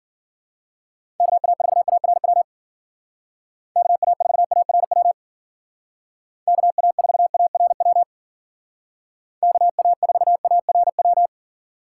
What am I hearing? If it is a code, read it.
Morse code, heard as a keyed tone, about 52 wpm, 700 Hz. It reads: KA4ARW KA4ARW KA4ARW KA4ARW